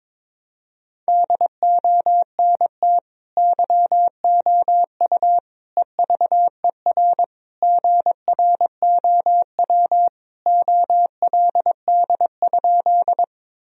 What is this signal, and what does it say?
Morse code, heard as a keyed tone, about 22 wpm, 700 Hz. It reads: DONT YOU EVER GROW OLD?
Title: Sentences from Top 1000 Words - Review Part 1 - 22wpm